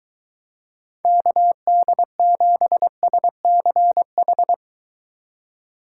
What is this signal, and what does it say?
Morse code, heard as a keyed tone, about 23 wpm, 700 Hz. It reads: KD7SCH